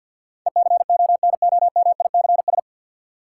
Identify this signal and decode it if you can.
Morse code, heard as a keyed tone, about 47 wpm, 700 Hz. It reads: ECONOMICS
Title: Sentences from Top 1000 Words - Review Part 3 - New Word Drill - 50wpm